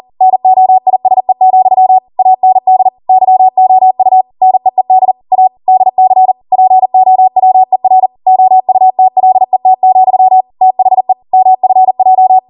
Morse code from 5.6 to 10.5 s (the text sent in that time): DC POWER OUTLET,